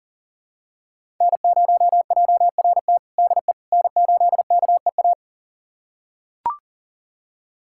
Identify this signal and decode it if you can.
Morse code, heard as a keyed tone, about 40 wpm, 700 Hz. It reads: N0JRT DE N8KEA
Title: USA Callsign DE Callsign - 5 Characters - 40wpm